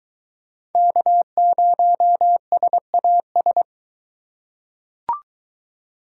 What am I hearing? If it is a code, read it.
Morse code, heard as a keyed tone, about 20 wpm, 700 Hz. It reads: K0SAS